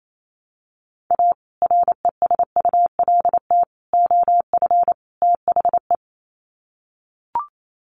Morse code, heard as a keyed tone, about 28 wpm, 700 Hz. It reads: A RESULT OF THE
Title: Sets of 4 Words 28wpm